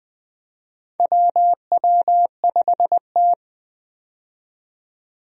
Morse code, heard as a keyed tone, about 20 wpm, 700 Hz. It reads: WW5T
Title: USA Call Signs - 4 Characters - 20wpm